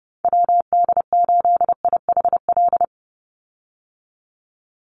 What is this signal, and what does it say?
Morse code, heard as a keyed tone, about 30 wpm, 700 Hz. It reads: WD8IHL
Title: USA Call Signs 30wpm